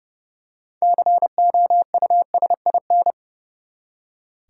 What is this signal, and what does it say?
Morse code, heard as a keyed tone, about 30 wpm, 700 Hz. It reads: COUSIN
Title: Six Letter Words - 30wpm